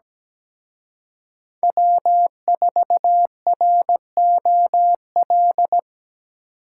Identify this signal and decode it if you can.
Morse code, heard as a keyed tone, about 20 wpm, 700 Hz. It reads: W4ROL
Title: Speed Racing - USA Call Signs 17wpm